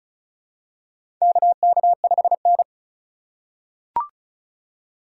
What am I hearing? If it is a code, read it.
Morse code, heard as a keyed tone, about 36 wpm, 700 Hz. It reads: KK5N
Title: USA Call Signs - 2x Character Spacing - 35wpm